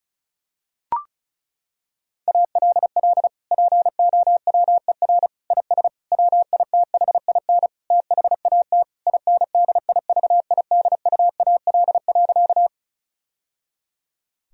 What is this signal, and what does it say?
Morse code, heard as a keyed tone, about 36 wpm, 700 Hz. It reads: ALL POWER IS WITHIN THAT INDIVIDUAL.
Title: Sentences from Top 1000 Words - Review Part 4 - 35wpm